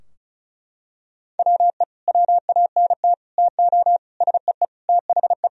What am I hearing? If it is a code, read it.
Morse code, heard as a keyed tone, about 36 wpm, 700 Hz. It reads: WE WANT TO SEE THE
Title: Sentences from Top 1000 Words - Review Part 2 - 35wpm